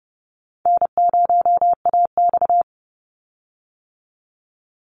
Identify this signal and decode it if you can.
Morse code, heard as a keyed tone, about 30 wpm, 700 Hz. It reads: N0AX
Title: Easy Call Signs - 30wpm